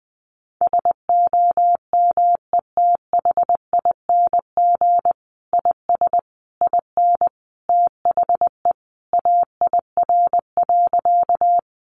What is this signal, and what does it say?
Morse code, heard as a keyed tone, about 20 wpm, 700 Hz. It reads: SOMETHING IS IN THE AIR.
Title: Sentences from Top 600 Words 20wpm